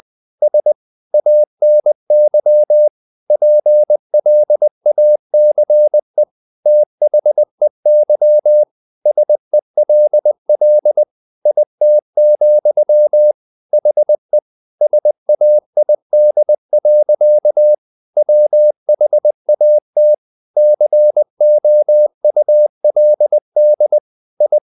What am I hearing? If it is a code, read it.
Morse code, heard as a keyed tone, about 20 wpm, 600 Hz. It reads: S ANY PLACE THEY SELL IT, HE SAID. WHAT COULD I
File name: the old man and the sea part 2 20wpm